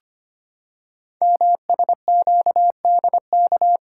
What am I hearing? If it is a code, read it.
Morse code, heard as a keyed tone, about 25 wpm, 700 Hz. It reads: MSQDK
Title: Code Groups - 5 Letters - 25wpm